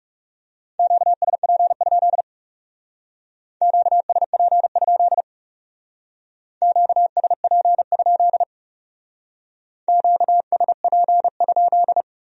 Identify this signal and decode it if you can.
Morse code, heard as a keyed tone, about 38 wpm, 700 Hz. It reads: QSP? QSP? QSP? QSP?